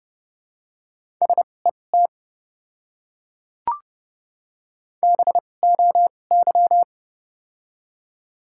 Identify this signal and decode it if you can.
Morse code, heard as a keyed tone, about 29 wpm, spaced as for 22 wpm, 700 Hz. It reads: SET BOY
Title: 2x Character Spacing - Top 200 Words - 30wpm